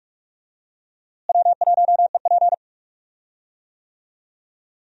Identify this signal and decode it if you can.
Morse code, heard as a keyed tone, about 46 wpm, 700 Hz. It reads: W1EP
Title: USA Call Signs - 4 Characters - Repeated 3x - 45wpm